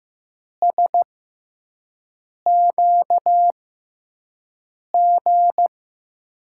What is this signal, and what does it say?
Morse code, heard as a keyed tone, about 15 wpm, spaced as for 5 wpm, 700 Hz. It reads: SQG
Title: USA Call Signs - 6 Characters - 6x Character Spacing - 15wpm